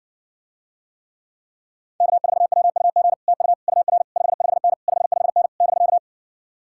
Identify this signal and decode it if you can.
Morse code, heard as a keyed tone, about 60 wpm, 700 Hz. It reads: K4WUG TU UR 55N 55N <BK>